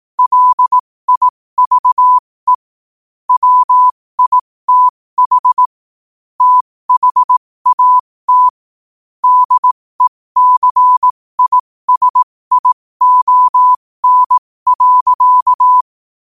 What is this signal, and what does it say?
Morse code, heard as a keyed tone, about 18 wpm, spaced as for 15 wpm, 1000 Hz. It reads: LIVE WITH THAT DECISION.
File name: qotdfiles / 15wpm